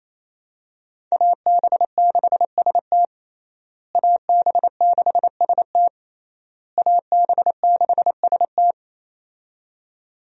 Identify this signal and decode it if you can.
Morse code, heard as a keyed tone, about 28 wpm, 700 Hz. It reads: AB6ST AB6ST AB6ST
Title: USA Call Signs - 5 Characters - Repeated 3x - 28wpm